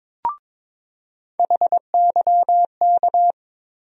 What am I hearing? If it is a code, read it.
Morse code, heard as a keyed tone, about 22 wpm, 700 Hz. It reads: HYK
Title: USA Call Sign Postfixes - 3 Letters 22wpm